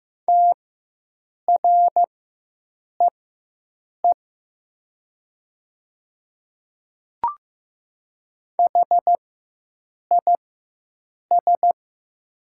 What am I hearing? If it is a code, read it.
Morse code, heard as a keyed tone, about 15 wpm, spaced as for 7 wpm, 700 Hz. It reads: TREE HIS
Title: 4x Character Spacing - Top 300 Words - 15wpm